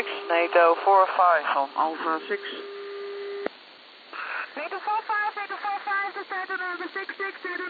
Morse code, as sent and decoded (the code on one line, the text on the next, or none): none